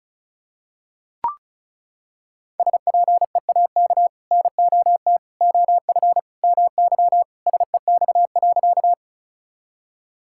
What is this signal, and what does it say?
Morse code, heard as a keyed tone, about 36 wpm, 700 Hz. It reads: SPEAK NOT OF MY SEX.
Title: Sentences from Top 1600 Words 35wpm